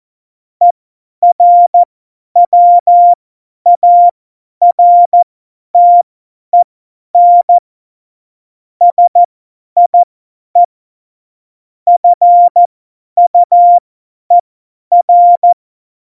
Morse code, heard as a keyed tone, about 14 wpm, spaced as for 10 wpm, 700 Hz. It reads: ERWARTEN SIE FUER
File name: Morsekurs / Witze / Witz 17. September 2019, 50 F70